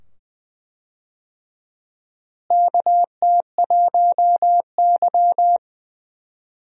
Morse code, heard as a keyed tone, about 20 wpm, 700 Hz. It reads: KT1Y
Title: USA Call Signs - 4 Characters - Repeated 2x - 20wpm